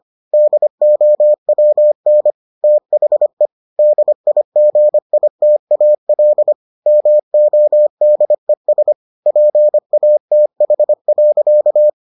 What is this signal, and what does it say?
Morse code, heard as a keyed tone, about 25 wpm, 600 Hz. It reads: DOWN THE DIGITAL MODES PATH.